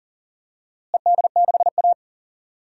Morse code, heard as a keyed tone, about 40 wpm, 700 Hz. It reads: ED6A
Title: DXCC-100 Call Signs - 4 Characters - Repeated 3x - 40wpm